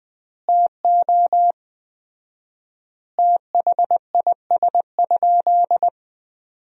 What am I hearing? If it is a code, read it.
Morse code, heard as a keyed tone, about 20 wpm, 700 Hz. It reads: TO THIS?
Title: 4x Word-Spacing - Sentences from Top 100 Words - 20wpm